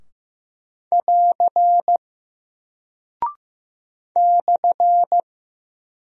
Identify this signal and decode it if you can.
Morse code, heard as a keyed tone, about 15 wpm, 700 Hz. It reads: <AR> /
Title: Single Character (Letter:Number:Punctuation:Prosign) Alt 15wpm